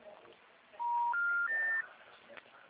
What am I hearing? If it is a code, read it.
Special information tone.
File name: out- -2011- - - .56